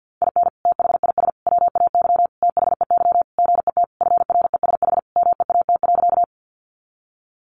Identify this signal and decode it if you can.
Morse code, heard as a keyed tone, about 50 wpm, 700 Hz. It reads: IN THIS WAY THEY GET FRESH MEAT.
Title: Sentences from Top 1200 Words 50wpm